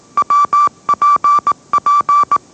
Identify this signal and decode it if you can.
Morse code, heard as a keyed tone, about 18 wpm, 1160 Hz. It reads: WPP